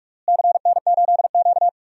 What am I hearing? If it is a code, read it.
Morse code, heard as a keyed tone, about 44 wpm, 700 Hz. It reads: CN8Q
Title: DXCC-100 Call Signs - 4 Characters - Repeated 2x - 45wpm